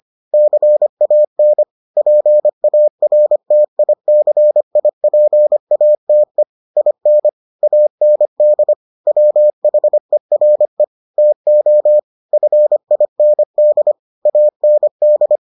Morse code, heard as a keyed tone, about 25 wpm, 600 Hz. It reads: CAN PARTICIPATE IN AND WHERE TO FIND AND